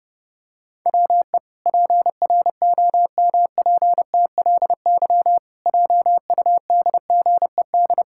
Morse code, heard as a keyed tone, about 30 wpm, 700 Hz. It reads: WE PROMPTLY JUDGED
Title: Pangram Challenge v2 - 30wpm